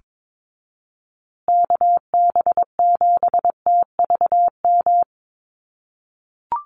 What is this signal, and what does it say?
Morse code, heard as a keyed tone, about 22 wpm, 700 Hz. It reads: KB7TVM
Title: USA Call Signs Repeated 3x 22wpm